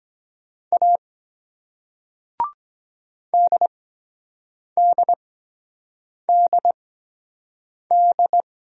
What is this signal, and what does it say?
Morse code, heard as a keyed tone, about 23 wpm, spaced as for 7 wpm, 700 Hz. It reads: A DDDD